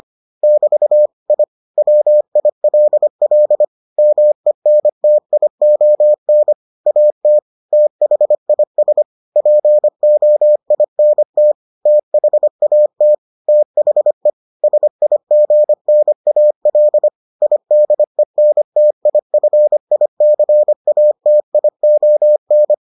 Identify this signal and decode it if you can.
Morse code, heard as a keyed tone, about 25 wpm, 600 Hz. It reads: <BT> I WILL MENTION AT THIS POINT THAT THE SIGNAL IDENTIFICATION